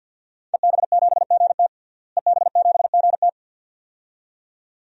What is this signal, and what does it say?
Morse code, heard as a keyed tone, about 46 wpm, 700 Hz. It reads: EB7GT EB7GT
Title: DXCC-100 Call Signs - 5 Characters - Repeated 2x - 50wpm